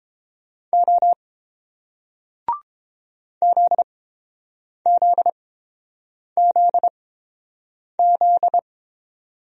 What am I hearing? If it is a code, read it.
Morse code, heard as a keyed tone, about 31 wpm, spaced as for 7 wpm, 700 Hz. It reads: O ZZZZ